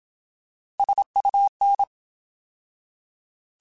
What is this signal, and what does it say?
Morse code, heard as a keyed tone, about 27 wpm, 800 Hz. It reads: SUN
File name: Alphabetical / Alphabetical mp3s / Ss @25